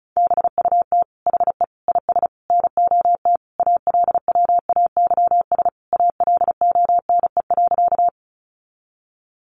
Morse code, heard as a keyed tone, about 34 wpm, 700 Hz. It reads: BUT HE IS NOT ALWAYS ALONE.
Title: Sentences from Top 1000 Words 35wpm